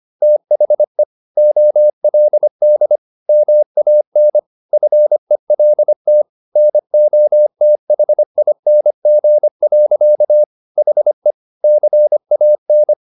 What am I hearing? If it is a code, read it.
Morse code, heard as a keyed tone, about 25 wpm, 600 Hz. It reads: THE OLD MAN FELT NOTHING. HE CAN
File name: the old man and the sea part 1 25wpm